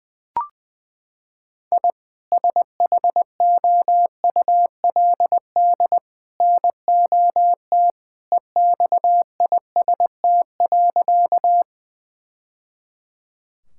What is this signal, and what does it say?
Morse code, heard as a keyed tone, about 20 wpm, 700 Hz. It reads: I SHOULD NOT EXIST.
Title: Sentences from Top 1000 Words - Review Part 5 - 20wpm